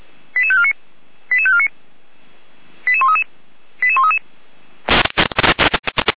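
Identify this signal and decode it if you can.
Five-tone selective call (ZVEI), heard as quick runs of five tones, tones of about 70 ms. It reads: 88549 88549 88133 88133
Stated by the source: ZVEI five-tone selective call received by SDR.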